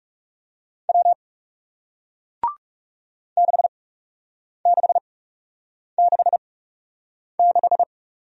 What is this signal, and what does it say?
Morse code, heard as a keyed tone, about 41 wpm, spaced as for 8 wpm, 700 Hz. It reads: W 6666